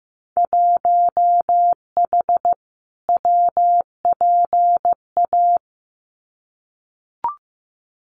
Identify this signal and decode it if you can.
Morse code, heard as a keyed tone, about 15 wpm, 700 Hz. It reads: 1H WPA
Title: Winter Field Day Exchanges 15wpm